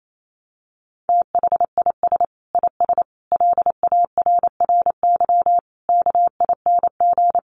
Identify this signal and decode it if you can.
Morse code, heard as a keyed tone, about 28 wpm, 700 Hz. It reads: THIS IS LARRY KING
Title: Sets of 5 Words Encore2 28wpm